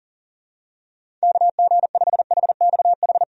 Morse code, heard as a keyed tone, about 40 wpm, 700 Hz. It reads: KG5HXH
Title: USA Call Signs - 6 Characters - 40wpm